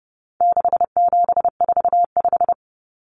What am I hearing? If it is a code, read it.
Morse code, heard as a keyed tone, about 30 wpm, 700 Hz. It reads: 6745
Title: Numbers - 4-Digit 30wpm